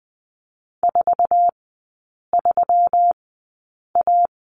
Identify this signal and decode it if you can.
Morse code, heard as a keyed tone, about 20 wpm, spaced as for 8 wpm, 700 Hz. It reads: 43A
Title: 191 Lesson 31e - Letter v - Three Characters 20wpm